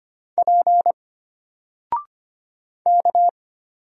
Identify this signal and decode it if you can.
Morse code, heard as a keyed tone, about 25 wpm, 700 Hz. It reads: P K
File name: Single-Letter-Number-25wpm